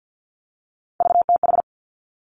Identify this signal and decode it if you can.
Morse code, heard as a keyed tone, about 50 wpm, 700 Hz. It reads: VTH
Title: Three Challenging Letter-Number Combinations 50wpm